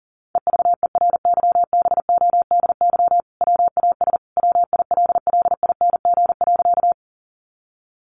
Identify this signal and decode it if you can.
Morse code, heard as a keyed tone, about 40 wpm, 700 Hz. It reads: EVERYBODY WAS WILLING.
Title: Sentences from Top 1400 Words 40wpm